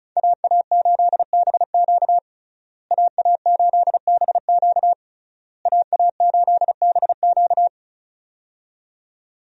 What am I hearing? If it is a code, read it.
Morse code, heard as a keyed tone, about 36 wpm, 700 Hz. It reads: AA8BQ AA8BQ AA8BQ
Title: USA Call Signs - 5 Characters - Repeated 3x - 35wpm